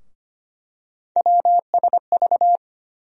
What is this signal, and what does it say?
Morse code, heard as a keyed tone, about 25 wpm, 700 Hz. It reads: WSV
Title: USA Call Sign Postfixes - 3 Letters 25wpm